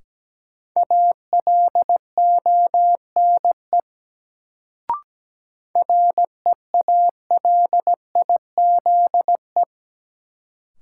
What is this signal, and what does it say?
Morse code, heard as a keyed tone, about 17 wpm, 700 Hz. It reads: ALONE REALIZE
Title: Sentences from Top 1000 Words - New Word Drill - 17wpm